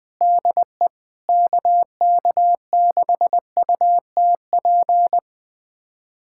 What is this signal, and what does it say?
Morse code, heard as a keyed tone, about 20 wpm, 700 Hz. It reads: DE KK6UTP
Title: USA Callsign DE Callsign - 6 Characters - 20wpm